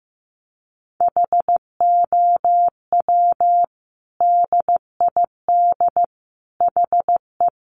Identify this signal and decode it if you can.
Morse code, heard as a keyed tone, about 15 wpm, 700 Hz. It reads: HOW DID HE
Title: Sentences from Top 1100 Words 15wpm